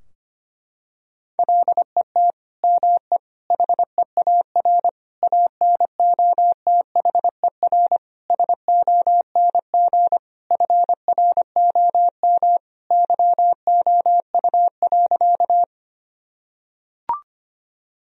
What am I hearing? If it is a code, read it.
Morse code, heard as a keyed tone, about 25 wpm, 700 Hz. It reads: LET ME HEAR ANOTHER SONG FROM YOU.
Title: Sentences from Top 1000 Words - Review Part 2 - 25wpm